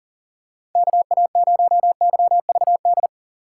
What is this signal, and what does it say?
Morse code, heard as a keyed tone, about 37 wpm, 700 Hz. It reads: KA0YVD
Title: USA Call Signs - 6 Characters - 3x Character Spacing - 40wpm